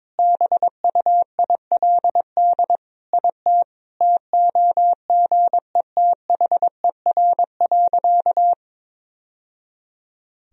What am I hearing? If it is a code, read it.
Morse code, heard as a keyed tone, about 21 wpm, 700 Hz. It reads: BUILD IT TOGETHER.
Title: Sentences from Top 1000 Words - Review Part 1 - 22wpm